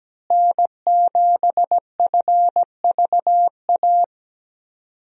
Morse code, heard as a keyed tone, about 17 wpm, 700 Hz. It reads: N7FVA
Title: USA Call Signs - 5 Characters - Repeated 2x - 17wpm